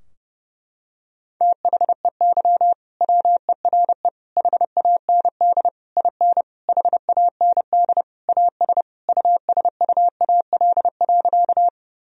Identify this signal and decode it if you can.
Morse code, heard as a keyed tone, about 32 wpm, 700 Hz. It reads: THEY WERE HAND IN HAND AS USUAL.
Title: Sentences from Top 1000 Words - Review Part 2 - 30wpm